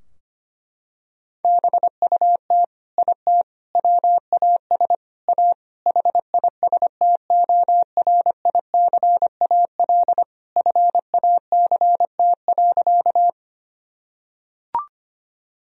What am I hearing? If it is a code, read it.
Morse code, heard as a keyed tone, about 25 wpm, 700 Hz. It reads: BUT IT WAS A HISTORICAL FACT.
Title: Sentences from Top 1000 Words - Review Part 4 - 25wpm